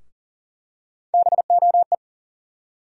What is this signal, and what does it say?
Morse code, heard as a keyed tone, about 40 wpm, 700 Hz. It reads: BOE
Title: USA Call Sign Postfixes - 3 Letters 40wpm